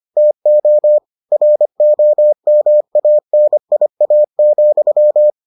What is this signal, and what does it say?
Morse code, heard as a keyed tone, about 25 wpm, 600 Hz. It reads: TO ROMANIA,